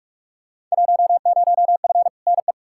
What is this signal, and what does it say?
Morse code, heard as a keyed tone, about 44 wpm, 700 Hz. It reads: 10F NE
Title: Field Day Exchanges Encore - 45wpm